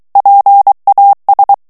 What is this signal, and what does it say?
Morse code, heard as a keyed tone, about 23 wpm, 800 Hz. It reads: PAS
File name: FRANCAIS TEXTE CLAIR 2 VITESSE 22 MOTS MINUTES